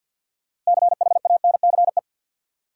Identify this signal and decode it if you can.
Morse code, heard as a keyed tone, about 52 wpm, 700 Hz. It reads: CHANCE
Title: Sets of 3 Words - New Word Drill - 50wpm